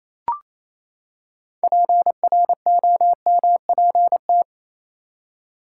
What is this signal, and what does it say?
Morse code, heard as a keyed tone, about 28 wpm, 700 Hz. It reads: PROMPT
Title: Six Letter Words - 28wpm